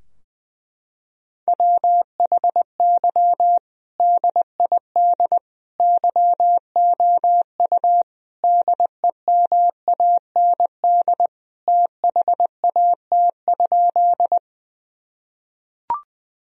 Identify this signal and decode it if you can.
Morse code, heard as a keyed tone, about 21 wpm, 700 Hz. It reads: WHY DID YOU DEMAND THAT?
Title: Sentences from Top 1000 Words - Review Part 4 - 20wpm